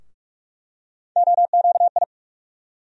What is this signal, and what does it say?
Morse code, heard as a keyed tone, about 42 wpm, 700 Hz. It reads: OQI
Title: USA Call Sign Postfixes - 3 Letters 45wpm